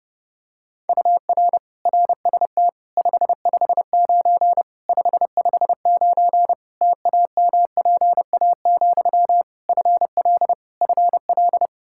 Morse code, heard as a keyed tone, about 32 wpm, 700 Hz. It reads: UR RST 559 559 TAMPA, FL FL
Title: Practice QSO - Exchange 1 - Beginning - 30wpm